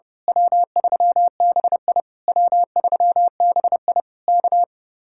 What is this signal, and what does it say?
Morse code, heard as a keyed tone, about 28 wpm, 700 Hz. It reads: W3BI W3BI K
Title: Speed-Racing Common QSO Phrases - 30wpm